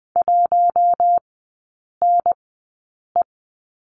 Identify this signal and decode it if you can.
Morse code, heard as a keyed tone, about 20 wpm, spaced as for 8 wpm, 700 Hz. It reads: 1NE
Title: 053 Lesson 10e - Letter r - Three Characters 20wpm